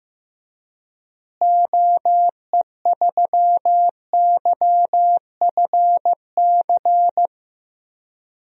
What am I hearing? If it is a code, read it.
Morse code, heard as a keyed tone, about 15 wpm, 700 Hz. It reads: OE3YFC
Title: DXCC-100 Call Signs - 6 Characters - Repeated 2x - 15wpm